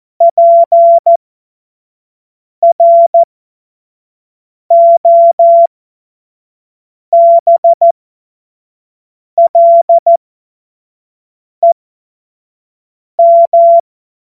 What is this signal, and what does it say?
Morse code, heard as a keyed tone, about 14 wpm, spaced as for 5 wpm, 680 Hz. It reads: PROBLEM
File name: Morsekurs / Witze / Witz 09. November 2021, 25 F70